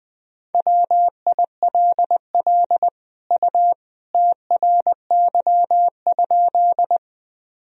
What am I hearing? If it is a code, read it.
Morse code, heard as a keyed tone, about 21 wpm, 700 Hz. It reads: WILL U TRY?
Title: Sentences from Top 200 Words and Abbreviations 20wpm